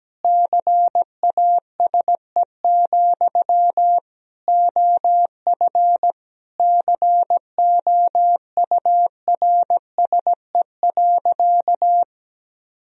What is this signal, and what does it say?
Morse code, heard as a keyed tone, about 17 wpm, 700 Hz. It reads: CASE, OF COURSE.
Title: Sentences from Top 1000 Words - Review Part 3 - 17wpm